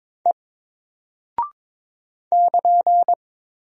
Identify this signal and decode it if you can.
Morse code, heard as a keyed tone, about 22 wpm, 700 Hz. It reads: E <KN>
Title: Single Character (Letter:Number:Punctuation:Prosign) 22wpm